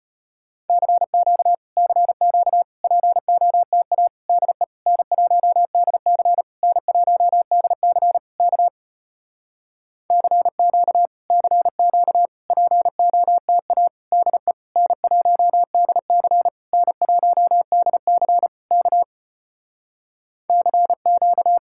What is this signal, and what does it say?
Morse code, heard as a keyed tone, about 33 wpm, 700 Hz. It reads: CQ CQ POTA DE N1DC N1DC K CQ CQ POTA DE N1DC N1DC K CQ